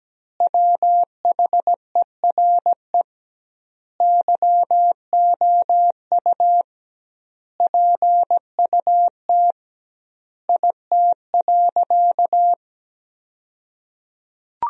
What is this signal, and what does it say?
Morse code, heard as a keyed tone, about 17 wpm, 700 Hz. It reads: WHERE YOU PUT IT.
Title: 2x Word-Spacing - Sentences from Top 200 Words - 17wpm